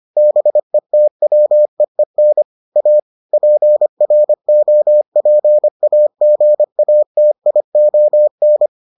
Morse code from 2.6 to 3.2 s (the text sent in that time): A